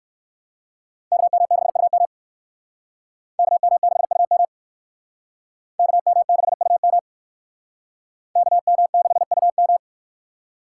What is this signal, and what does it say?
Morse code, heard as a keyed tone, about 56 wpm, 700 Hz. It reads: KM6UM KM6UM KM6UM KM6UM